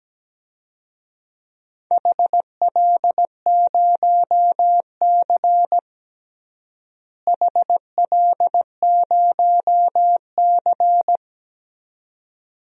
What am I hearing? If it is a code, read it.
Morse code, heard as a keyed tone, about 17 wpm, 700 Hz. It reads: HL0C HL0C
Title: DXCC-100 Call Signs - 4 Characters - Repeated 2x - 17wpm